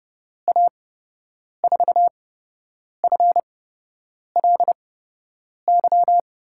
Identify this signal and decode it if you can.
Morse code, heard as a keyed tone, about 30 wpm, spaced as for 8 wpm, 700 Hz. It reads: A4FLY
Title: USA Call Signs - 6 Characters - 8x Character Spacing - 30wpm